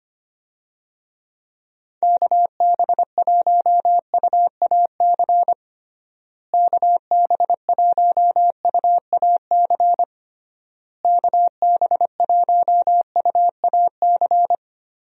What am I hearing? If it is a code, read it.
Morse code, heard as a keyed tone, about 25 wpm, 700 Hz. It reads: KB1UAC KB1UAC KB1UAC
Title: USA Call Signs - 6 Characters - Repeated 3x - 25wpm